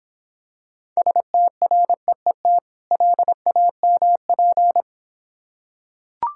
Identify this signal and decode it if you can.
Morse code, heard as a keyed tone, about 26 wpm, 700 Hz. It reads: STREET LAMP